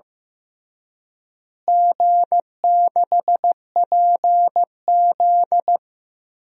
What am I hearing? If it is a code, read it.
Morse code, heard as a keyed tone, about 15 wpm, 700 Hz. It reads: G6PZ
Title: DXCC-100 Call Signs - 4 Characters - Repeated 3x - 15wpm